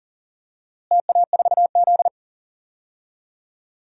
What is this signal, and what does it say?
Morse code, heard as a keyed tone, about 40 wpm, 700 Hz. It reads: TA4Z
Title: DXCC-100 Call Signs - 4 Characters - Repeated 2x - 40wpm